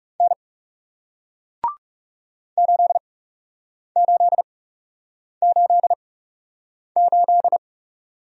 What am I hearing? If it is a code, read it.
Morse code, heard as a keyed tone, about 40 wpm, spaced as for 8 wpm, 700 Hz. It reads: N 8888